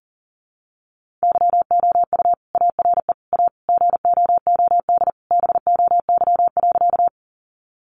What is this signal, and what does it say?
Morse code, heard as a keyed tone, about 40 wpm, 700 Hz. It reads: YOU ARE A GOOD BOY.
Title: Sentences from Top 200 Words 40wpm